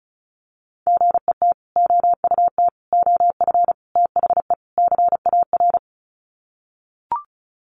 Morse code, heard as a keyed tone, about 34 wpm, 700 Hz. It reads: GET OUT OF THE CAR
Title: Sets of 5 Words Encore2 35wpm